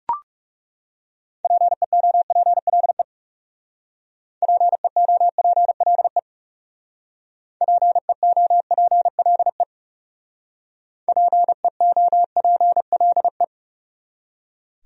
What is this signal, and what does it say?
Morse code, heard as a keyed tone, about 40 wpm, 700 Hz. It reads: PEOPLE PEOPLE PEOPLE PEOPLE